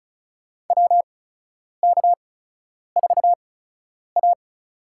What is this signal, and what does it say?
Morse code, heard as a keyed tone, about 36 wpm, spaced as for 10 wpm, 700 Hz. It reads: WK4A
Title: USA Call Signs - 4 Characters - 8x Character Spacing - 35wpm